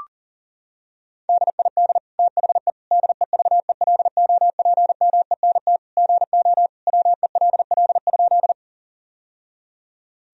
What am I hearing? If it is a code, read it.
Morse code, heard as a keyed tone, about 40 wpm, 700 Hz. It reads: DID THE DEVELOPMENT GO WELL?
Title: Sentences from Top 1000 Words - Review Part 4 - 40wpm